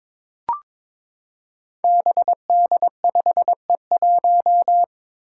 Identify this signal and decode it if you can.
Morse code, heard as a keyed tone, about 22 wpm, 700 Hz. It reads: BD5E1
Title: Five Challenging Letter-Number Combinations 22wpm